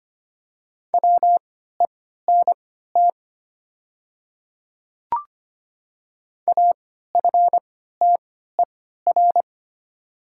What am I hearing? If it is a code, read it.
Morse code, heard as a keyed tone, about 25 wpm, spaced as for 14 wpm, 700 Hz. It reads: WENT AFTER
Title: 3x Character Spacing - Top 300 Words - 25wpm